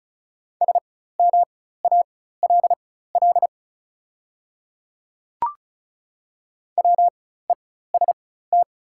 Morse code, heard as a keyed tone, about 36 wpm, spaced as for 16 wpm, 700 Hz. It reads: SMALL WEST
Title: 4x Character Spacing - Top 500 Words - 35wpm